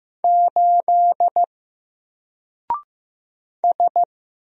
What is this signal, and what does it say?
Morse code, heard as a keyed tone, about 15 wpm, 700 Hz. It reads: 8 S
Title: Single Letter-Number 15wpm